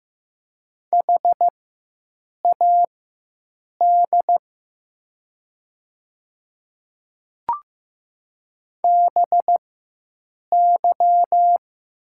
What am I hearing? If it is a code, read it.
Morse code, heard as a keyed tone, about 15 wpm, spaced as for 7 wpm, 700 Hz. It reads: HAD BY
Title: 4x Character Spacing - Top 100 Words - 15wpm